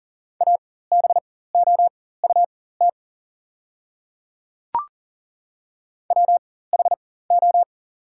Morse code, heard as a keyed tone, about 39 wpm, spaced as for 18 wpm, 690 Hz. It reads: ABOUT WHO
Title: 4x Character Spacing - Top 100 Words - 40wpm